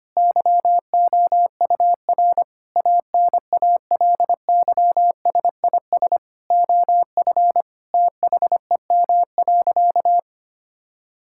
Morse code, heard as a keyed tone, about 25 wpm, 700 Hz. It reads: YOUR ANALYSIS OF THEM.
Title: Sentences from Top 1000 Words - Review Part 5 - 25wpm